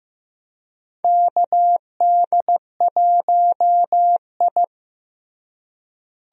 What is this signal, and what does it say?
Morse code, heard as a keyed tone, about 15 wpm, 700 Hz. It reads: KD1I
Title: USA Call Signs - 4 Characters - Repeated 3x - 15wpm